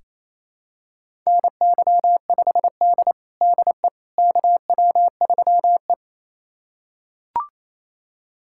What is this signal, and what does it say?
Morse code, heard as a keyed tone, about 27 wpm, 690 Hz. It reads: NY5D DE KW3E
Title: USA Callsign DE Callsign - 4 Characters - 28wpm